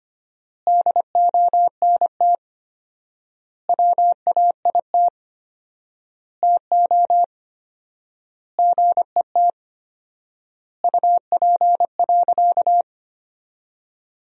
Morse code, heard as a keyed tone, about 25 wpm, 700 Hz. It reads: DONT WAIT TO GET UP.
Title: 4x Word-Spacing - Sentences from Top 500 Words - 25wpm